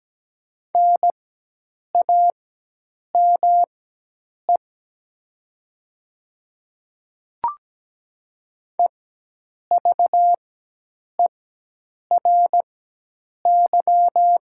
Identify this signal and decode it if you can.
Morse code, heard as a keyed tone, about 17 wpm, spaced as for 8 wpm, 700 Hz. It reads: NAME EVERY
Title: 4x Character Spacing - Top 300 Words - 17wpm